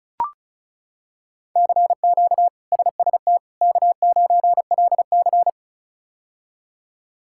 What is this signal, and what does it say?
Morse code, heard as a keyed tone, about 36 wpm, 700 Hz. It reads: CQ SST K9LC
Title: K1USN SST QSO Practice - 35wpm